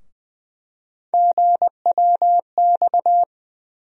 Morse code, heard as a keyed tone, about 20 wpm, 700 Hz. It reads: GWX
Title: USA Call Sign Postfixes - 3 Letters 20wpm